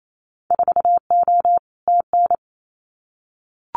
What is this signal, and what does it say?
Morse code, heard as a keyed tone, about 28 wpm, 700 Hz. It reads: 4O TN
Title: Winter Field Day Exchanges Encore 28wpm